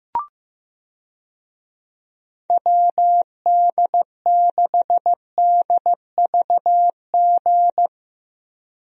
Morse code, heard as a keyed tone, about 15 wpm, 700 Hz. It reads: WD6DVG
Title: USA Call Signs - 6 Characters - 15wpm